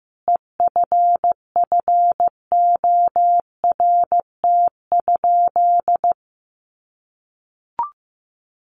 Morse code, heard as a keyed tone, about 15 wpm, 700 Hz. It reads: EFFORT?
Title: Sentences from Top 1400 Words 15wpm